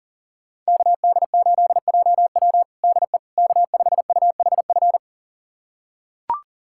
Morse code, heard as a keyed tone, about 40 wpm, 700 Hz. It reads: KD8JW DE K5UHF
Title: USA Callsign DE Callsign - 5 Characters - 40wpm